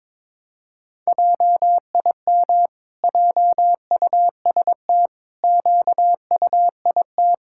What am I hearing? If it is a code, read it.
Morse code, heard as a keyed tone, about 21 wpm, 700 Hz. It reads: JIM JUST QUIT
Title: Pangram Challenge v2 - 22wpm